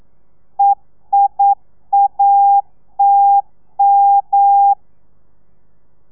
Morse code, heard as a keyed tone, about 9 wpm, 800 Hz. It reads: EIATM
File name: telegrafia / i2tfj / Lez2 40